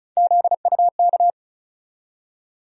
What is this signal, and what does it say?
Morse code, heard as a keyed tone, about 36 wpm, 700 Hz. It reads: ZUK
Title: USA Call Sign Postfixes - 3 Letters 35wpm